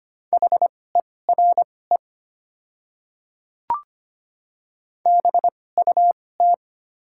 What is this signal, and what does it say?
Morse code, heard as a keyed tone, about 25 wpm, spaced as for 18 wpm, 700 Hz. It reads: HERE BUT
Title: 2x Character Spacing - Top 300 Words - 25wpm